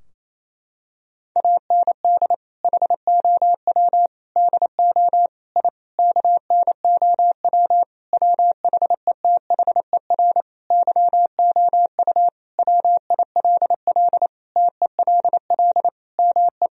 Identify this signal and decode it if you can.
Morse code, heard as a keyed tone, about 28 wpm, 700 Hz. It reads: AND HOW DO I KNOW WHETHER YOU WILL TELL ME
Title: Sentences from Top 1000 Words - Review Part 3 - 28wpm